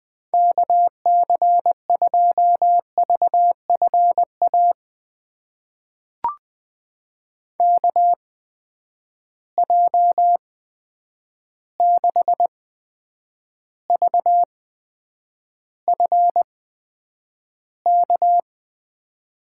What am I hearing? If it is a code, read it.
Morse code, heard as a keyed tone, about 20 wpm, spaced as for 9 wpm, 700 Hz. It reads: KC2VFA KJ6VFK
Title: USA Call Signs - 6 Characters - 8x Character Spacing - 20wpm